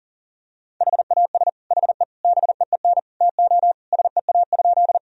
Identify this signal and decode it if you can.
Morse code, heard as a keyed tone, about 40 wpm, 700 Hz. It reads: HAS HE BEEN TO SEA?
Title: Sentences from Top 1000 Words - Review Part 1 - 40wpm